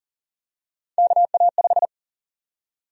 Morse code, heard as a keyed tone, about 38 wpm, 700 Hz. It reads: KA5
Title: USA Call Sign Prefixes - 40wpm